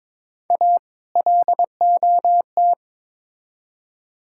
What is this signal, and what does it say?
Morse code, heard as a keyed tone, about 22 wpm, 700 Hz. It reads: A LOT
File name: Build A Phrase - 2-Word Phrases - 22wpm